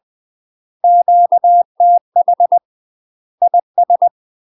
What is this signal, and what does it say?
Morse code, heard as a keyed tone, about 20 wpm, 700 Hz. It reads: QTH IS
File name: UK MASTER QSOs TRAINING / 20WPM / UK MASTER QSOs 0025